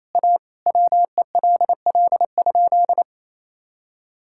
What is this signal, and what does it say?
Morse code, heard as a keyed tone, about 28 wpm, 700 Hz. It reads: A WELL?
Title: Sentences from Top 1000 Words - Review Part 6 - 28wpm